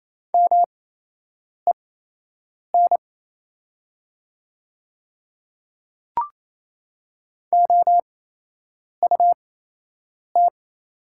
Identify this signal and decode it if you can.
Morse code, heard as a keyed tone, about 28 wpm, spaced as for 8 wpm, 700 Hz. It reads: MEN OUT
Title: 8x Character Spacing - Top 200 Words - 28wpm